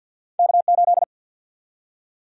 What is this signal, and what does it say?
Morse code, heard as a keyed tone, about 51 wpm, 700 Hz. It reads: K8
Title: USA Call Sign Prefixes - 50wpm